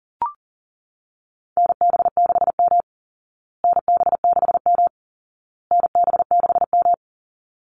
Morse code, heard as a keyed tone, about 40 wpm, 700 Hz. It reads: NB6M NB6M NB6M
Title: Call Signs Repeated 2x - 40wpm